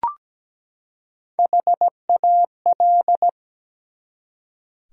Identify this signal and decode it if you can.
Morse code, heard as a keyed tone, about 17 wpm, 700 Hz. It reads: HAL
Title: USA Common Names - Max 5 Letters - 17wpm